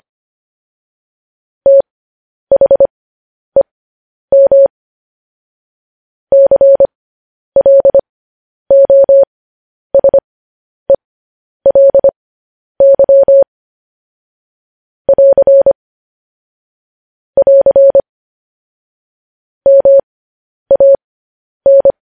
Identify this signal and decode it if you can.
Morse code, heard as a keyed tone, about 25 wpm, spaced as for 10 wpm, 550 Hz. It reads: THEM CLOSELY <AR> <AR> MAN
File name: podcast / News headlines-1x-25x10